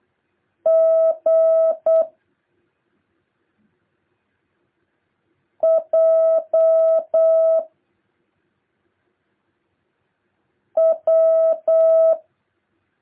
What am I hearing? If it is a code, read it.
Morse code, heard as a keyed tone, about 8 wpm, 650 Hz. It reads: G J W
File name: jh8cbh / 2023cw